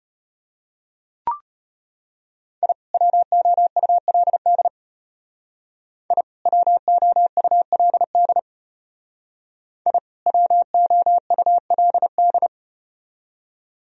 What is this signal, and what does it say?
Morse code, heard as a keyed tone, about 33 wpm, 700 Hz. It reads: I WOULD I WOULD I WOULD